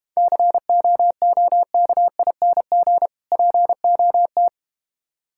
Morse code, heard as a keyed tone, about 31 wpm, 700 Hz. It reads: COOKING POT
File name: practice / 25wpm / Difficult Words - Speed-Racing with Hints - 25wpm